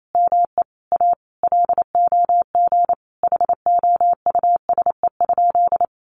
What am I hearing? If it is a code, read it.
Morse code, heard as a keyed tone, about 28 wpm, 700 Hz. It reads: ME A LOG HOUSE?
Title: Sentences from Top 1400 Words 28wpm